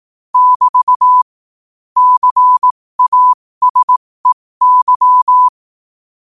Morse code, heard as a keyed tone, about 18 wpm, spaced as for 15 wpm, 1000 Hz. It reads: <BT> CASEY
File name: qotdfiles / 15wpm